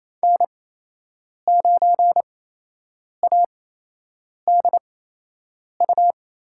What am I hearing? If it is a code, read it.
Morse code, heard as a keyed tone, about 28 wpm, spaced as for 8 wpm, 700 Hz. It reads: N9ADU
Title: USA Call Signs - 8x Character Spacing - 28wpm